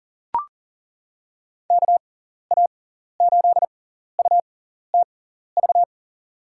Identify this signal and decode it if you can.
Morse code, heard as a keyed tone, about 40 wpm, spaced as for 14 wpm, 700 Hz. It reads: KA8UTV
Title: USA Call Signs - 6 Characters - 6x Character Spacing - 40wpm